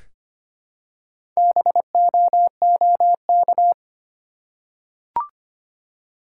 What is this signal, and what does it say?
Morse code, heard as a keyed tone, about 25 wpm, 700 Hz. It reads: BOOK
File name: practice / 25wpm / Easy Words with Hints - 25wpm